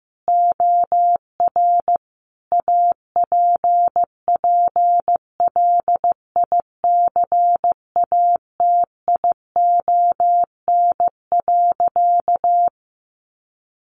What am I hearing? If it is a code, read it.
Morse code, heard as a keyed tone, about 15 wpm, 700 Hz. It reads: OR APPLICATION.
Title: Sentences from Top 1200 Words 15wpm